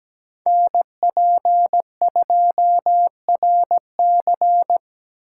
Morse code, heard as a keyed tone, about 17 wpm, 700 Hz. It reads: NP2RC
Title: USA Call Signs - 5 Characters - 6x Character Spacing - 17wpm